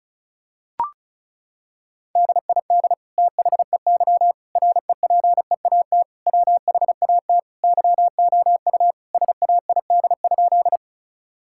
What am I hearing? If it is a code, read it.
Morse code, heard as a keyed tone, about 34 wpm, 700 Hz. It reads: DID THEY REPEAT WHAT YOU SAID?
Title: Sentences from Top 1600 Words 35wpm